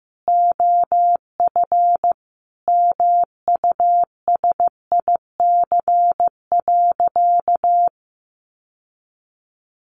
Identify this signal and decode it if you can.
Morse code, heard as a keyed tone, about 14 wpm, 700 Hz. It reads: OF MUSIC.
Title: Sentences from Top 1200 Words 15wpm